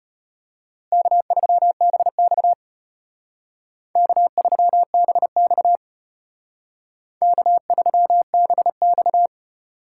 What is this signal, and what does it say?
Morse code, heard as a keyed tone, about 33 wpm, 700 Hz. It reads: K3BX K3BX K3BX